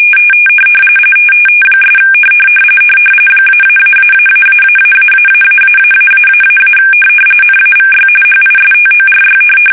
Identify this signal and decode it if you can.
RTTY, about 45.45 baud. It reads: ' -------------------- NAVY HISTORY
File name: fsk850 / USS-Massachusetts